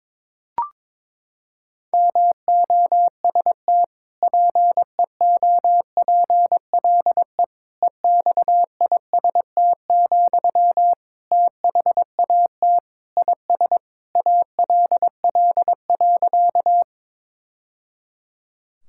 Morse code, heard as a keyed tone, about 22 wpm, 700 Hz. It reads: MOST PEOPLE EXIST, THAT IS ALL.
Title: Sentences from Top 1000 Words - Review Part 5 - 22wpm